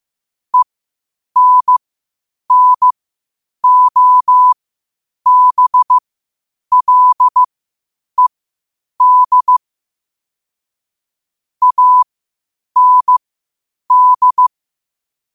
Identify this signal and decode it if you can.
Morse code, heard as a keyed tone, about 15 wpm, spaced as for 8 wpm, 1000 Hz. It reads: ENNOBLED AND
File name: qotdfiles / 8wpm